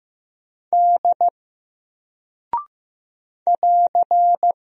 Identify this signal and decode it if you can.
Morse code, heard as a keyed tone, about 15 wpm, 700 Hz. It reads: D <AR>
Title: Single Character (Letter:Number:Punctuation:Prosign) Alt 15wpm